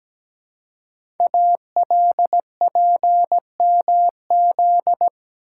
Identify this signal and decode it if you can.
Morse code, heard as a keyed tone, about 17 wpm, 700 Hz. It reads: ALPMZ
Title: Code Groups - 5 Letter:Numbers - 17wpm